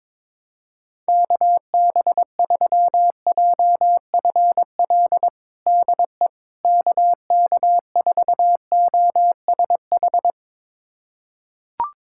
Morse code, heard as a keyed tone, about 22 wpm, 700 Hz. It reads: KB3JFL DE KK4OSH
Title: USA Callsign DE Callsign 22wpm